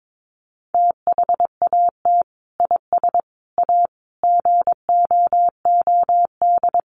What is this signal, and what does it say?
Morse code, heard as a keyed tone, about 22 wpm, 700 Hz. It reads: THAT IS A GOOD
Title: Sentences from Top 900 Words 22wpm